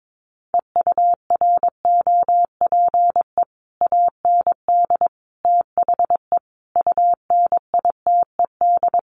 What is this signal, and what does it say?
Morse code, heard as a keyed tone, about 22 wpm, 700 Hz. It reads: EUROPE AND THE UNITED
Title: Sets of 5 Words Encore2 22wpm